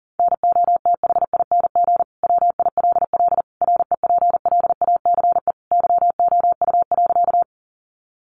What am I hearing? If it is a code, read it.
Morse code, heard as a keyed tone, about 37 wpm, 700 Hz. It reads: NOTHING WILL REPLACE YOU.
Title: Sentences from Top 1400 Words 40wpm